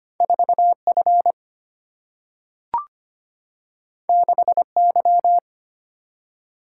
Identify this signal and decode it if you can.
Morse code, heard as a keyed tone, about 25 wpm, 700 Hz. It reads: 4 F 6 Y
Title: Single-Letter Single-Number 25wpm